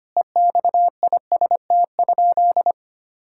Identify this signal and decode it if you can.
Morse code, heard as a keyed tone, about 25 wpm, 700 Hz. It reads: EXIST?
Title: Sentences from Top 1000 Words - Review Part 5 - 25wpm